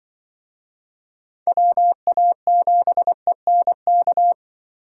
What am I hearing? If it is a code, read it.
Morse code, heard as a keyed tone, about 25 wpm, 700 Hz. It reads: WA7ENK
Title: Speed Racing - USA 2x3 Call Signs 20wpm